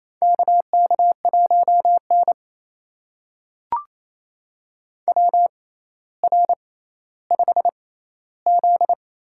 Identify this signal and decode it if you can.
Morse code, heard as a keyed tone, about 28 wpm, spaced as for 14 wpm, 700 Hz. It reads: KK1N WR5Z
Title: USA Call Signs - 4 Characters - 6x Character Spacing - 28wpm